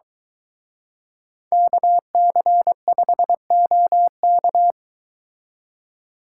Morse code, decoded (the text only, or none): KC5OK